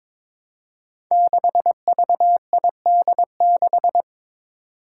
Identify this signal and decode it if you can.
Morse code, heard as a keyed tone, about 22 wpm, 700 Hz. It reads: 6VID6
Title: Five Challenging Letter-Number Combinations 22wpm